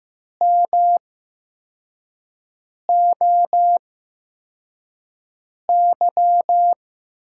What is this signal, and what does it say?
Morse code, heard as a keyed tone, about 15 wpm, spaced as for 4 wpm, 700 Hz. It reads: MOY
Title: USA Call Signs - 6 Characters - 8x Character Spacing - 15wpm